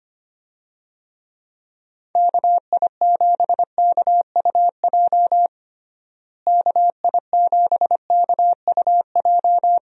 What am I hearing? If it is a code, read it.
Morse code, heard as a keyed tone, about 25 wpm, 700 Hz. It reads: KI7KUJ KI7KUJ
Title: USA Call Signs - 6 Characters - Repeated 3x - 25wpm